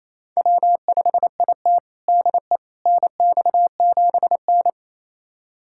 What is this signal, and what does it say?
Morse code, heard as a keyed tone, about 28 wpm, 700 Hz. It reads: W5IT DE NX7N
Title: USA Callsign DE Callsign - 4 Characters - 28wpm